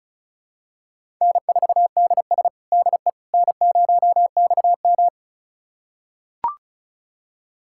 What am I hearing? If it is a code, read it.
Morse code, heard as a keyed tone, about 36 wpm, 700 Hz. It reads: N4DS DE N0XM
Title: USA Callsign DE Callsign - 4 Characters - 35wpm